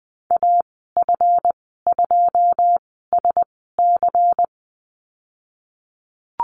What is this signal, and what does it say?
Morse code, heard as a keyed tone, about 20 wpm, spaced as for 15 wpm, 700 Hz. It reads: AF2SC
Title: 173 Lesson 28g - Number 9 - Call Signs 20wpm